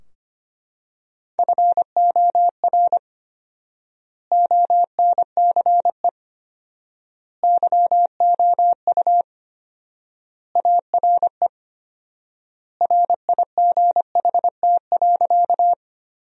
Morse code, heard as a keyed tone, about 25 wpm, 700 Hz. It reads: FOR ONCE YOU ARE RIGHT.
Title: 4x Word-Spacing - Sentences from Top 300 Words - 25wpm